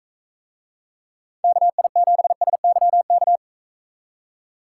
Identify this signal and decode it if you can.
Morse code, heard as a keyed tone, about 40 wpm, 700 Hz. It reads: KI7SYK